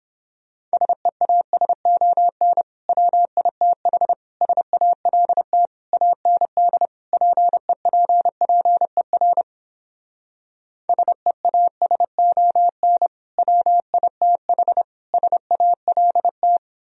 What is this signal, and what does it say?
Morse code, heard as a keyed tone, about 29 wpm, 700 Hz. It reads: SEASON WITH SALT AND PEPPER SEASON WITH SALT